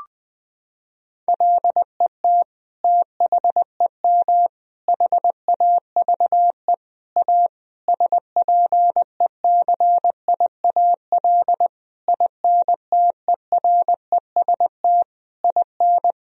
Morse code, heard as a keyed tone, about 21 wpm, 700 Hz. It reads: LET THEM HAVE A SPECIAL INTEREST IN
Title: Sentences from Top 1000 Words - Review Part 3 - 20wpm